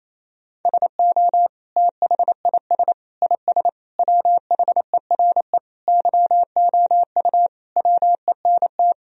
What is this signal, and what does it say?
Morse code, heard as a keyed tone, about 28 wpm, 700 Hz. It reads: SO THIS IS WHERE YOU WENT
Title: Sentences from Top 1000 Words - Review Part 2 - 28wpm